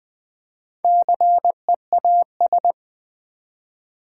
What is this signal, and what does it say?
Morse code, heard as a keyed tone, about 20 wpm, 700 Hz. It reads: CEAS
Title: Build A Word - 1 Letter At A Time - 3 to 5-Letter Words - 20wpm